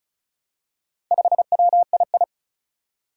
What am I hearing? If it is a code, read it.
Morse code, heard as a keyed tone, about 36 wpm, 700 Hz. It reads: 5WII
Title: Four Challenging Letter-Number Combinations 35wpm